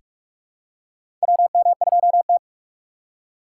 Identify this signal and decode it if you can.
Morse code, heard as a keyed tone, about 44 wpm, 700 Hz. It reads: WM2T
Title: USA Call Signs - 4 Characters - 6x Character Spacing - 45wpm